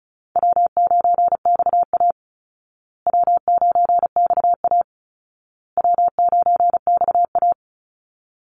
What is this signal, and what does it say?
Morse code, heard as a keyed tone, about 34 wpm, 700 Hz. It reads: W9XA W9XA W9XA
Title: Easy Call Signs Repeated 3x - 35wpm